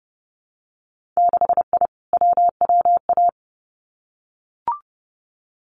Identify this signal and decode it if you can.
Morse code, heard as a keyed tone, about 30 wpm, 700 Hz. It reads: 6I WWA
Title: Winter Field Day Exchanges Encore 30wpm